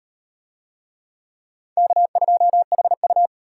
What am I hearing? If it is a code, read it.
Morse code, heard as a keyed tone, about 34 wpm, 700 Hz. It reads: K2HU